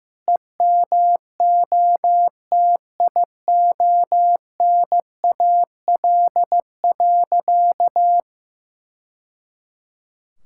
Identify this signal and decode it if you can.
Morse code, heard as a keyed tone, about 15 wpm, 700 Hz. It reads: EMOTIONAL.
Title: Sentences from Top 1000 Words - Review Part 5 - 15wpm